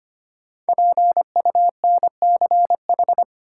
Morse code, heard as a keyed tone, about 25 wpm, 700 Hz. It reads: PUNCH
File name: 5 Letter Words - First 3 Letters Given - 25wpm